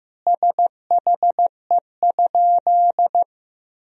Morse code, heard as a keyed tone, about 15 wpm, 700 Hz. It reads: SHE?
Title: 2x Word-Spacing - Sentences from Top 100 Words - 15wpm